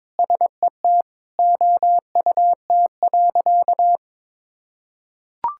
Morse code, heard as a keyed tone, about 22 wpm, 700 Hz. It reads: SET OUT.
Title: Sentences from Top 1600 Words 22wpm